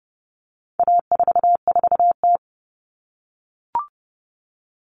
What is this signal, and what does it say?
Morse code, heard as a keyed tone, about 30 wpm, 700 Hz. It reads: A44T
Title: All Cut Numbers - 4-Digit 30wpm